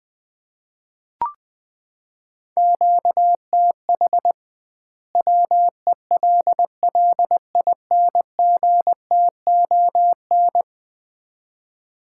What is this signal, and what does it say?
Morse code, heard as a keyed tone, about 20 wpm, 700 Hz. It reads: QTH WELLINGTON
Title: Common QSO Phrases - 2x Word-Spacing - 20wpm